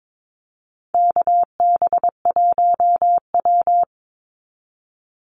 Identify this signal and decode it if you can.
Morse code, heard as a keyed tone, about 22 wpm, 690 Hz. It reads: KB1W
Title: Call Signs Repeated 3x - 22wpm